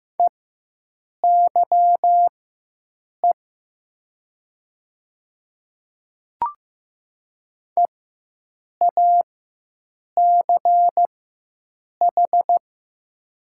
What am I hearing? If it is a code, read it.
Morse code, heard as a keyed tone, about 15 wpm, spaced as for 7 wpm, 700 Hz. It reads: EYE EACH
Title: 4x Character Spacing - Top 500 Words - 15wpm